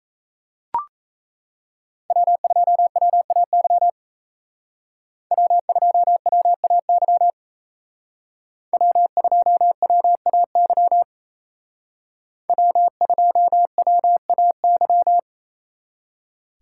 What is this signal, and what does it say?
Morse code, heard as a keyed tone, about 35 wpm, 700 Hz. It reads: W2WAY W2WAY W2WAY W2WAY
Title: Speed Racing - USA 1x3 Call Signs 28wpm